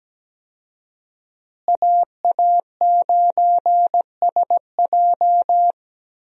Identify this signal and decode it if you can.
Morse code, heard as a keyed tone, about 17 wpm, 700 Hz. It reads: AA9SJ
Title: USA Call Signs - 5 Characters - Repeated 2x - 17wpm